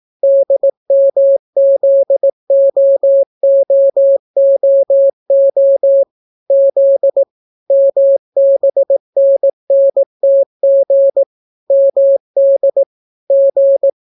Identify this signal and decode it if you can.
Morse code, heard as a keyed tone, about 18 wpm, 550 Hz. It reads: DMZOOOO Z MBNNTG MD G